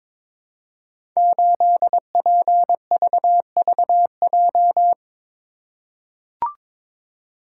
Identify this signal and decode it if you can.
Morse code, heard as a keyed tone, about 22 wpm, 700 Hz. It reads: 8PVVJ
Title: Code Groups - 5 Letter:Numbers - 22wpm